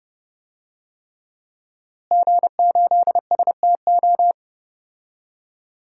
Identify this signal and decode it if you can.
Morse code, heard as a keyed tone, about 30 wpm, 700 Hz. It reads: G8STO
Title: DXCC-100 Call Signs - 5 Characters - Repeated 3x - 30wpm